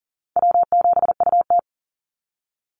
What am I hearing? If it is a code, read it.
Morse code, heard as a keyed tone, about 37 wpm, 700 Hz. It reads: W7UT
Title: Call Signs Repeated 3x - 40wpm